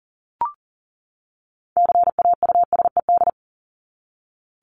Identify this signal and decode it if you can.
Morse code, heard as a keyed tone, about 40 wpm, 700 Hz. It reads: CAUSED
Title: ED Suffix - Top 100 Words with Base Words 40wpm